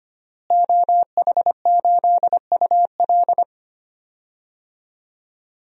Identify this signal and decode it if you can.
Morse code, heard as a keyed tone, about 25 wpm, 700 Hz. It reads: OH8UL
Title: DXCC-100 Call Signs - 5 Characters - Repeated 2x - 25wpm